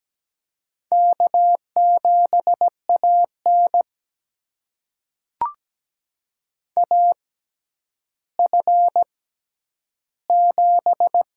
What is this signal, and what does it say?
Morse code, heard as a keyed tone, about 17 wpm, spaced as for 10 wpm, 700 Hz. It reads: K7AN AF7
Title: USA Call Signs - 4 Characters - 6x Character Spacing - 17wpm